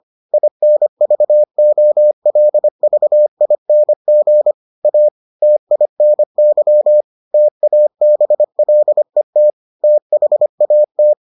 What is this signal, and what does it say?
Morse code, heard as a keyed tone, about 25 wpm, 600 Hz. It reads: INVOLVING A TINY TABLET THAT